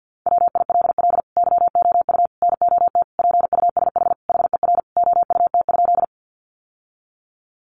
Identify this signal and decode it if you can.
Morse code, heard as a keyed tone, about 50 wpm, 700 Hz. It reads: WILL YOU NOT PUSH HER OUT?
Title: Sentences from Top 1300 Words 50wpm